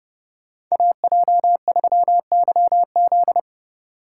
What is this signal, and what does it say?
Morse code, heard as a keyed tone, about 39 wpm, 700 Hz. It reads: AJ3YZ